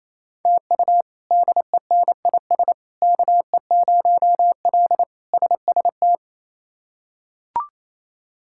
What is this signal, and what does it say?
Morse code, heard as a keyed tone, about 28 wpm, 700 Hz. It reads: TU DENIS KE0L SST
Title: K1USN SST QSO Practice - 28wpm